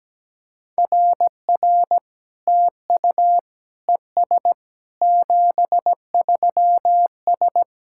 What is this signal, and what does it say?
Morse code, heard as a keyed tone, about 17 wpm, 700 Hz. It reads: RR TU ES 73S
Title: SOTA QSO Practice 17wpm